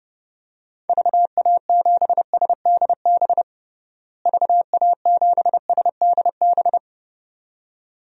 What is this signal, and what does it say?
Morse code, heard as a keyed tone, about 32 wpm, 700 Hz. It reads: VA7SDB VA7SDB
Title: DXCC-100 Call Signs - 6 Characters - Repeated 2x - 30wpm